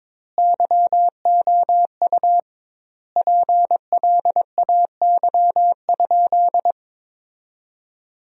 Morse code, heard as a keyed tone, about 22 wpm, 700 Hz. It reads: YOU PLAY?
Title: 2x Word-Spacing - Sentences from Top 300 Words - 22wpm